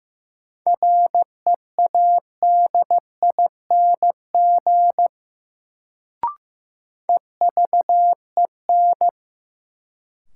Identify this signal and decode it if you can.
Morse code, heard as a keyed tone, about 15 wpm, 700 Hz. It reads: READING EVEN
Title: Sentences from Top 600 Words - New Word Drill - 15wpm